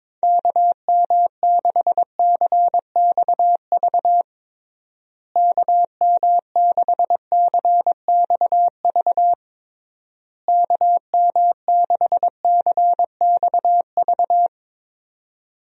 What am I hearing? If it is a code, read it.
Morse code, heard as a keyed tone, about 22 wpm, 700 Hz. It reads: KM6CXV KM6CXV KM6CXV
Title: USA Call Signs - 6 Characters - Repeated 3x - 22wpm